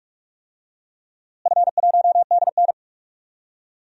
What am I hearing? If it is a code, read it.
Morse code, heard as a keyed tone, about 42 wpm, 700 Hz. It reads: F1DN
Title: DXCC-100 Call Signs - 4 Characters - Repeated 3x - 45wpm